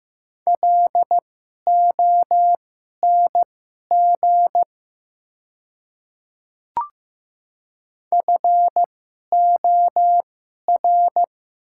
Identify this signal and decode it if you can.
Morse code, heard as a keyed tone, about 15 wpm, spaced as for 11 wpm, 700 Hz. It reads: LONG FOR
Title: 2x Character Spacing - Top 200 Words - 15wpm